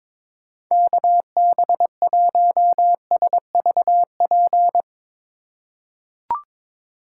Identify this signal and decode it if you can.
Morse code, heard as a keyed tone, about 22 wpm, 700 Hz. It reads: KB1SVP
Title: USA Call Signs - 6 Characters - 2x Character Spacing - 22wpm